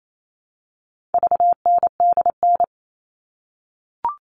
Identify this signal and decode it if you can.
Morse code, heard as a keyed tone, about 28 wpm, 700 Hz. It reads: VNDN
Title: All Cut Numbers - 4-Digit 28wpm